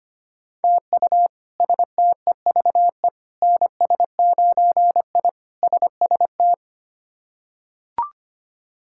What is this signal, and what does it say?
Morse code, heard as a keyed tone, about 25 wpm, 700 Hz. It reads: TU STEVE NS9I SST
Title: K1USN SST QSO Practice - 25wpm